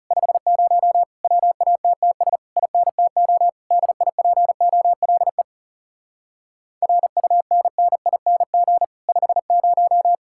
Morse code, heard as a keyed tone, about 34 wpm, 700 Hz. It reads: 50 WATTS INTO DIPOLE RUNNING 50
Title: Speed-Racing Common QSO Phrases Encore - 30wpm